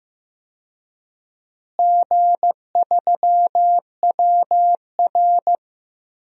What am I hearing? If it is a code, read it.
Morse code, heard as a keyed tone, about 15 wpm, 700 Hz. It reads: G3WR
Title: DXCC-100 Call Signs - 4 Characters - Repeated 3x - 15wpm